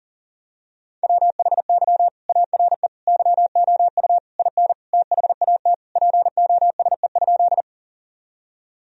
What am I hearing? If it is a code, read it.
Morse code, heard as a keyed tone, about 40 wpm, 700 Hz. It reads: WHY ARE YOU IN THAT POSE?
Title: Sentences from Top 1000 Words - Review Part 2 - 40wpm